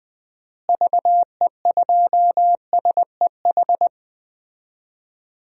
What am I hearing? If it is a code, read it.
Morse code, heard as a keyed tone, about 20 wpm, 700 Hz. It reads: VE2SEH
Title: DXCC-100 Call Signs - 6 Characters - Repeated 2x - 20wpm